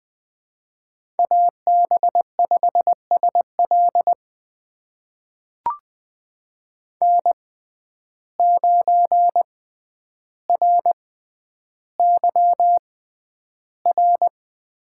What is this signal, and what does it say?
Morse code, heard as a keyed tone, about 20 wpm, spaced as for 10 wpm, 700 Hz. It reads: AB5SL N9RYR
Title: USA Call Signs - 5 Characters - 6x Character Spacing - 20wpm